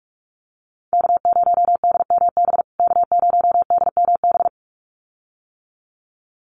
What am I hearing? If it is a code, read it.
Morse code, heard as a keyed tone, about 44 wpm, 700 Hz. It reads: K0DMB K0DMB
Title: USA Call Signs Repeated 2x 45wpm